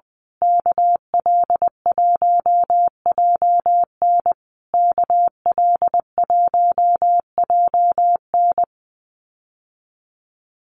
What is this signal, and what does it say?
Morse code, heard as a keyed tone, about 20 wpm, 700 Hz. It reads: KL1JN KL1JN
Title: USA Call Signs Repeated 3x 20wpm